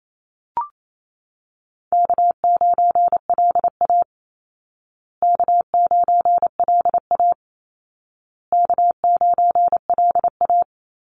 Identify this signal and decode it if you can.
Morse code, heard as a keyed tone, about 28 wpm, 700 Hz. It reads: K9LA K9LA K9LA
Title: Easy Call Signs Repeated 3x - 28wpm